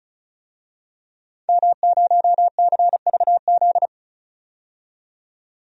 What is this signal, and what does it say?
Morse code, heard as a keyed tone, about 36 wpm, 700 Hz. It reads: M0CVZ
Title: DXCC-100 Call Signs - 5 Characters - 35wpm